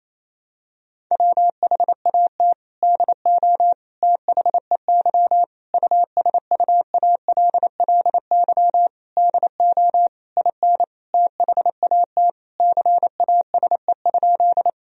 Morse code, heard as a keyed tone, about 28 wpm, 700 Hz. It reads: WHAT DO THEY USUALLY DO IN THAT CASE?
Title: Sentences from Top 1000 Words - Review Part 3 - 28wpm